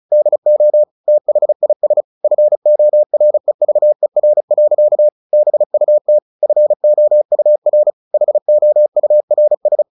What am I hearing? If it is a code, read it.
Morse code, heard as a keyed tone, about 34 wpm, 600 Hz. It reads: DO THIS FOREVER. BUT FOUR HOURS
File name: the old man and the sea / the old man and the sea 1 35wpm